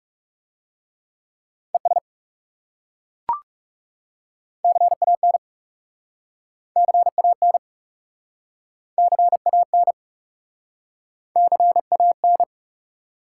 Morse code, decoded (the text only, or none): ES CAN CAN CAN CAN